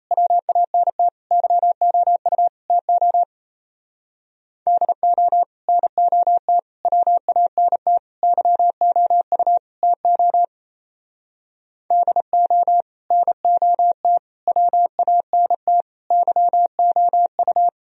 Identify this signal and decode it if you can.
Morse code, heard as a keyed tone, about 35 wpm, 700 Hz. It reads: WANT YOU TO DO NOT WANT YOU TO DO NOT WANT YOU
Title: Speed-Racing - Sets of 5 Words 28wpm